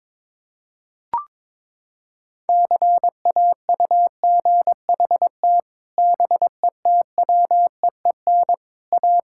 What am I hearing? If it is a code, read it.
Morse code, heard as a keyed tone, about 22 wpm, 700 Hz. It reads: CAUGHT BETWEEN A
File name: practice / 22wpm / Sets of 8 Words - 22wpm